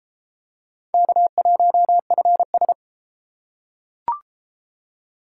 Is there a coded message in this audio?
Yes: Morse code from 0.9 to 2.7 s, heard as a keyed tone, about 29 wpm, 700 Hz. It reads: K1FS